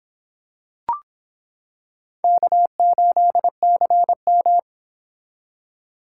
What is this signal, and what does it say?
Morse code, heard as a keyed tone, about 25 wpm, 700 Hz. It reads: K8CM